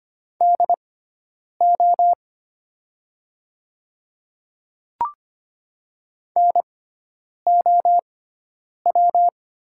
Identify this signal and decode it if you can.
Morse code, heard as a keyed tone, about 25 wpm, spaced as for 9 wpm, 700 Hz. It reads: DO NOW
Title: 6x Character Spacing - Top 100 Words - 25wpm